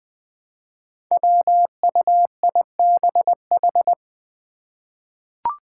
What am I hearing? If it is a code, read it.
Morse code, heard as a keyed tone, about 20 wpm, 700 Hz. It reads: WUIBH
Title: Five Challenging Letter-Number Combinations 20wpm